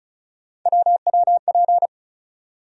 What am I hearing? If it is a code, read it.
Morse code, heard as a keyed tone, about 36 wpm, 700 Hz. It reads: WWP
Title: USA Call Sign Postfixes - 3 Letters 35wpm